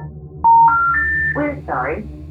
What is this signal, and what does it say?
Special information tone, with speech.
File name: songs / Dialup